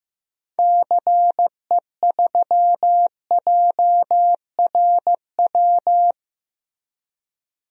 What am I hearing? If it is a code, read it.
Morse code, heard as a keyed tone, about 15 wpm, 700 Hz. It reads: CE3JRW
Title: DXCC-100 Call Signs - 6 Characters - Repeated 2x - 15wpm